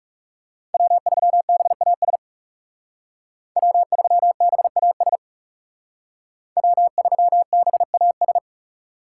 Morse code, heard as a keyed tone, about 36 wpm, 700 Hz. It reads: W3BAS W3BAS W3BAS
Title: Speed Racing - USA 1x3 Call Signs 30wpm